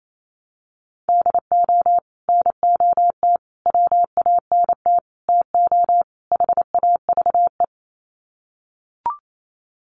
Morse code, heard as a keyed tone, about 28 wpm, 700 Hz. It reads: DO NOT WANT TO HAVE
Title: Sets of 5 Words 28wpm